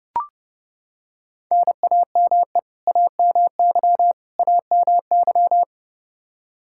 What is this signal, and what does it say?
Morse code, heard as a keyed tone, about 30 wpm, 700 Hz. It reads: NAME AMY AMY
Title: Practice QSO - Exchange 1 - Beginning - 30wpm